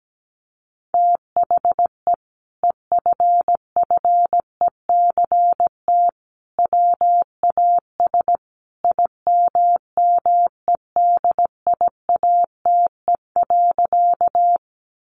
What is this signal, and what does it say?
Morse code, heard as a keyed tone, about 17 wpm, 700 Hz. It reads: THE EFFECT WAS IMMEDIATE.
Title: Sentences from Top 1200 Words 17wpm